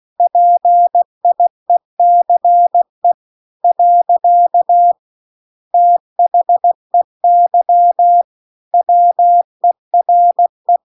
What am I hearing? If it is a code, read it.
Morse code, heard as a keyed tone, about 16 wpm, 700 Hz. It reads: PIECE . THEY WERE
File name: CW-The-Big-Twoheaded-River-2-16WPM